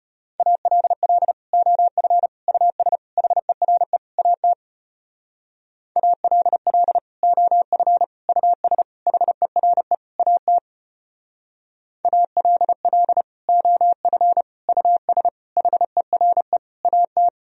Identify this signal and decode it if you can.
Morse code, heard as a keyed tone, about 32 wpm, 700 Hz. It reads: ALL OF US HERE AT ALL OF US HERE AT ALL OF US HERE AT